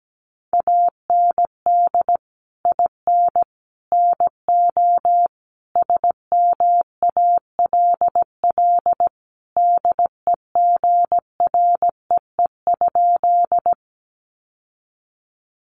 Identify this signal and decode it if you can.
Morse code, heard as a keyed tone, about 17 wpm, 700 Hz. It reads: AND IN NO SMALL DEGREE?
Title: Sentences from Top 1500 Words 17wpm